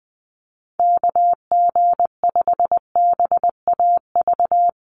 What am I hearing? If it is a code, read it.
Morse code, heard as a keyed tone, about 20 wpm, 700 Hz. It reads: KG5BAV
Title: USA Call Signs Repeated 2x 20wpm